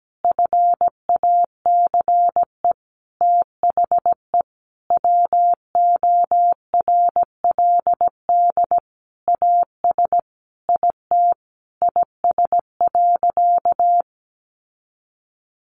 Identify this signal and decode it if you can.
Morse code, heard as a keyed tone, about 17 wpm, 700 Hz. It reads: FACE THE WORLD AS IT IS.
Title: Sentences from Top 1200 Words 17wpm